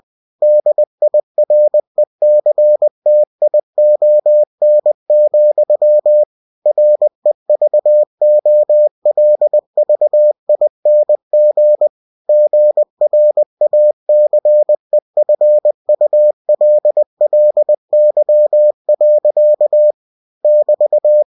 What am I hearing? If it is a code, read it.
Morse code, heard as a keyed tone, about 20 wpm, 600 Hz. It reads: DIRECTION, REVOLVING GRACEFULLY. <BT>
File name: cw-books / winnie-the-pooh / winnie-20wpm-page-0100